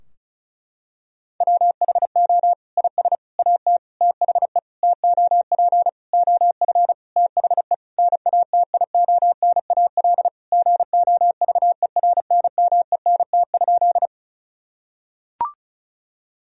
Morse code, heard as a keyed tone, about 36 wpm, 700 Hz. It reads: WHO IS AT THE TOP OF THE NATIONAL GOVERNMENT?
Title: Sentences from Top 1000 Words - Review Part 4 - 35wpm